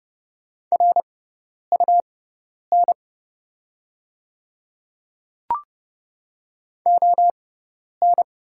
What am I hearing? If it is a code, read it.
Morse code, heard as a keyed tone, about 30 wpm, spaced as for 10 wpm, 700 Hz. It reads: RUN ON
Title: 6x Character Spacing - Top 300 Words - 30wpm